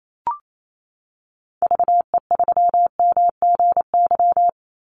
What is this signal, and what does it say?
Morse code, heard as a keyed tone, about 28 wpm, 700 Hz. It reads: VE3MGY
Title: Call Signs Repeated 3x - 28wpm